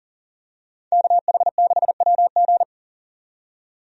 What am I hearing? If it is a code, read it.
Morse code, heard as a keyed tone, about 37 wpm, 700 Hz. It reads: KH6WG
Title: DXCC-100 Call Signs - 5 Characters - 40wpm